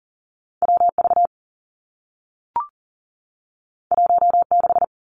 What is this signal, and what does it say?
Morse code, heard as a keyed tone, about 38 wpm, 700 Hz. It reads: W V 1 6
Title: Two Challenging Letter-Number Combinations 40wpm